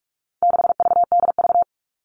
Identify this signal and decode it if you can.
Morse code, heard as a keyed tone, about 44 wpm, 700 Hz. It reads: 6VDV
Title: All Cut Numbers - 4-Digit 45wpm